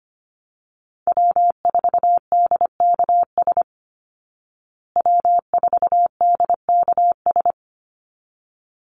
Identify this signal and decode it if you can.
Morse code, heard as a keyed tone, about 25 wpm, 700 Hz. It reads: W4DKS W4DKS
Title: Call Signs Repeated 3x - 25wpm